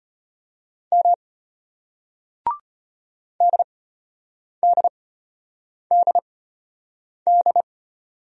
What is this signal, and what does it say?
Morse code, heard as a keyed tone, about 34 wpm, spaced as for 8 wpm, 700 Hz. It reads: M DDDD